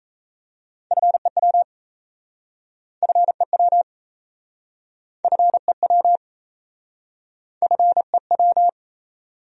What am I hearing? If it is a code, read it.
Morse code, heard as a keyed tone, about 38 wpm, 700 Hz. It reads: FEW FEW FEW FEW